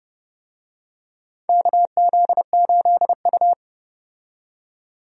Morse code, heard as a keyed tone, about 32 wpm, 700 Hz. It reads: KZ8U
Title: USA Call Signs - 4 Characters - 30wpm